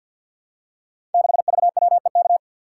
Morse code, heard as a keyed tone, about 53 wpm, 700 Hz. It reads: 6VWEK
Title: Code Groups - 5 Letter:Numbers - 50wpm